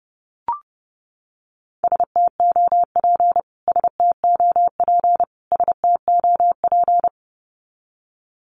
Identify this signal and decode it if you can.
Morse code, heard as a keyed tone, about 30 wpm, 700 Hz. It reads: STOP STOP STOP
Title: Top 500 Words Repeated 30wpm